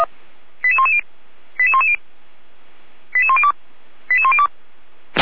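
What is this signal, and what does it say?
Five-tone selective call (ZVEI), heard as quick runs of five tones, tones of about 80 ms. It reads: 88110 88110 88102 88102
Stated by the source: ZVEI five-tone call 88119 received on an SDR.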